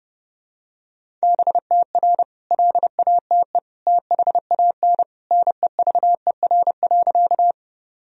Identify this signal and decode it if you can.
Morse code, heard as a keyed tone, about 30 wpm, 700 Hz. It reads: BTR LATE THAN NEVER.
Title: Sentences from Top 400 Words and Abbreviations 30wpm